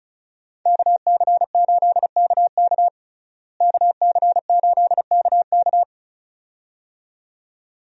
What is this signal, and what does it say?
Morse code, heard as a keyed tone, about 36 wpm, 700 Hz. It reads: KC8KK KC8KK
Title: USA Call Signs - 5 Characters - Repeated 2x - 35wpm